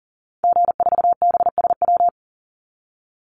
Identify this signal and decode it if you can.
Morse code, heard as a keyed tone, about 40 wpm, 700 Hz. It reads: G4BSW
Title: Call Signs Repeated 2x - 40wpm